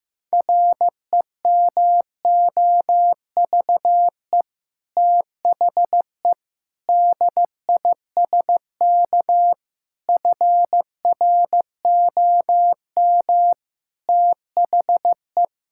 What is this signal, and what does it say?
Morse code, heard as a keyed tone, about 15 wpm, 700 Hz. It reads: REMOVE THE DISK FROM THE
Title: Sentences from Top 1000 Words - Review Part 6 - 15wpm